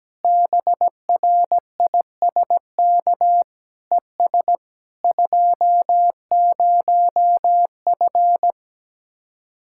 Morse code, heard as a keyed tone, about 17 wpm, 700 Hz. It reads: BRISK ES 20F
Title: Speed-Racing Common QSO Phrases Encore - 17wpm